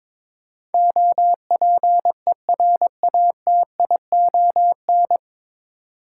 Morse code, heard as a keyed tone, about 22 wpm, 700 Hz. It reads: OPERATION
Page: Morse Code Ninja practice 22wpm